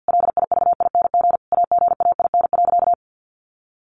Morse code, heard as a keyed tone, about 50 wpm, 700 Hz. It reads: LIVING AGAIN.
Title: Sentences from Top 1500 Words 50wpm